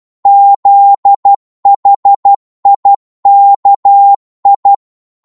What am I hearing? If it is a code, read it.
Morse code, heard as a keyed tone, about 12 wpm, 800 Hz. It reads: ZHIKI
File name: Morse / ltr-12wpm